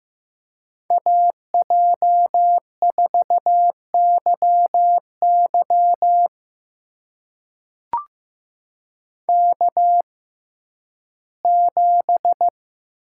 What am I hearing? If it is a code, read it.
Morse code, heard as a keyed tone, about 16 wpm, spaced as for 11 wpm, 700 Hz. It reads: AJ4YY K7
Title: USA Call Signs - 6x Character Spacing - 15wpm